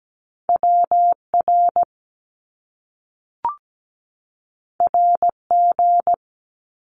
Morse code, heard as a keyed tone, about 17 wpm, 700 Hz. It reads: WR RG
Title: Top Two Letter Combinations in Words 17wpm